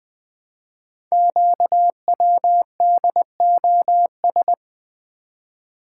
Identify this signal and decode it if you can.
Morse code, heard as a keyed tone, about 20 wpm, 700 Hz. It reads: QWDOS
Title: Code Groups - 5 Letters - 20wpm